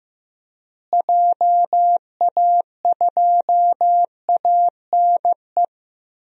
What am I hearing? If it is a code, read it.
Morse code, heard as a keyed tone, about 15 wpm, 700 Hz. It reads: JA2ANE
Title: DXCC-100 Call Signs - 6 Characters - Repeated 3x - 15wpm